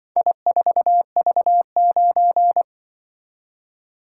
Morse code, heard as a keyed tone, about 23 wpm, 700 Hz. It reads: I4V9
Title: Speed-Racing - Four Letter-Number Combinations - 20wpm